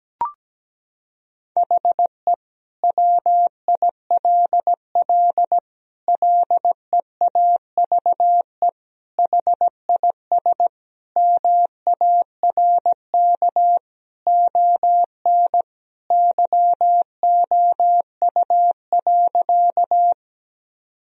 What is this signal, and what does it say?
Morse code, heard as a keyed tone, about 17 wpm, 700 Hz. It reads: HE WILL LEAVE HIS MARK ON YOU.
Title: Sentences from Top 1000 Words - Review Part 1 - 17wpm